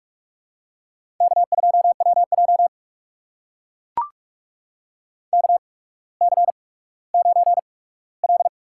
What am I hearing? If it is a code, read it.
Morse code, heard as a keyed tone, about 44 wpm, spaced as for 19 wpm, 700 Hz. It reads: K2WJ KC9L
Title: USA Call Signs - 8x Character Spacing - 45wpm